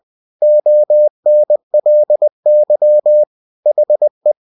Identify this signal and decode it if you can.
Morse code, heard as a keyed tone, about 20 wpm, 600 Hz. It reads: ONLY HE
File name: cw-books / winnie-the-pooh / winnie-20wpm-page-0000